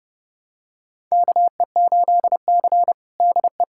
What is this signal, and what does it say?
Morse code, heard as a keyed tone, about 30 wpm, 700 Hz. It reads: KE8C DE
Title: USA Callsign DE Callsign - 4 Characters - 30wpm